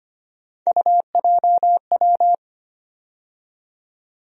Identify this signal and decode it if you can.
Morse code, heard as a keyed tone, about 25 wpm, 700 Hz. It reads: UJW
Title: USA Call Sign Postfixes - 3 Letters 25wpm